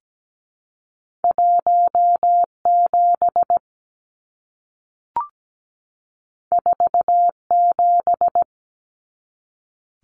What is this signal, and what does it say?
Morse code, heard as a keyed tone, about 17 wpm, 700 Hz. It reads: 17 47
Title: Numbers - 2-Digit 17wpm